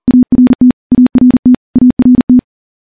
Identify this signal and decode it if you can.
Morse code, heard as a keyed tone, about 40 wpm, 250 Hz. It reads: ART ART ART
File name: morse / morse-training-40WPM